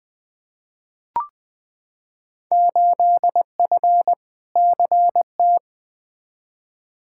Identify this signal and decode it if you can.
Morse code, heard as a keyed tone, about 20 wpm, 700 Hz. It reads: 8F CT
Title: Field Day Exchanges Encore - 20wpm